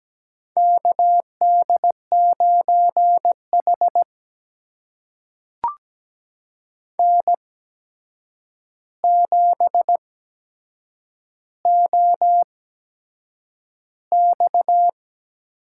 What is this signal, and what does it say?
Morse code, heard as a keyed tone, about 17 wpm, spaced as for 7 wpm, 700 Hz. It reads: KD9H N7OX
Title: USA Call Signs - 4 Characters - 8x Character Spacing - 17wpm